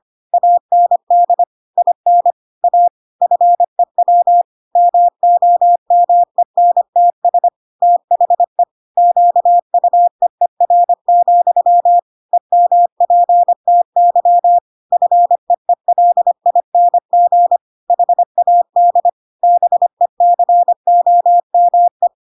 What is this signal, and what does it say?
Morse code, heard as a keyed tone, about 25 wpm, 700 Hz. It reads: AND IN A FEW MOMENTS THE QUEER, EMPTY FEELING HAD BECOME
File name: Pinocchio / Pinocchio ch5 25 25 ik1hge